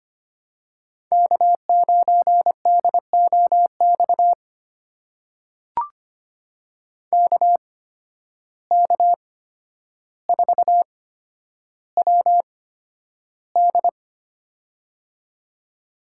Morse code, heard as a keyed tone, about 25 wpm, spaced as for 11 wpm, 700 Hz. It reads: K9DOX KK4WD
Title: USA Call Signs - 5 Characters - 8x Character Spacing - 25wpm